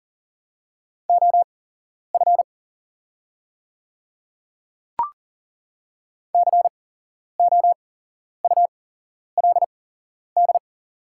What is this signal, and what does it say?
Morse code, heard as a keyed tone, about 40 wpm, spaced as for 11 wpm, 700 Hz. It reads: OF COULD
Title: 8x Character Spacing - Top 100 Words - 40wpm